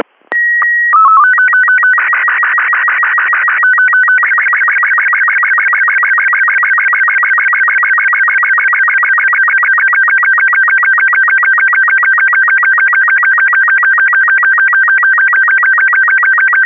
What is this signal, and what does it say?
SSTV (slow-scan television), Robot 36.